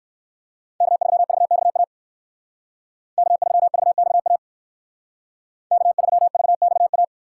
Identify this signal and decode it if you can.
Morse code, heard as a keyed tone, about 68 wpm, 700 Hz. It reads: K3VXA K3VXA K3VXA